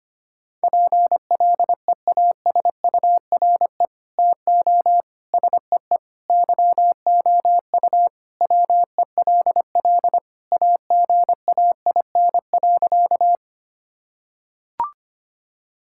Morse code, heard as a keyed tone, about 25 wpm, 700 Hz. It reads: PLEASURE TO SEE YOU WELL AGAIN.
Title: Sentences from Top 1600 Words 25wpm